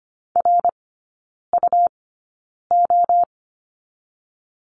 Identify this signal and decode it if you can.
Morse code, heard as a keyed tone, about 25 wpm, 700 Hz. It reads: R U O
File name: course / 25:10wpm / 100 Lesson 17e - Letter u - Three Characters 25wpm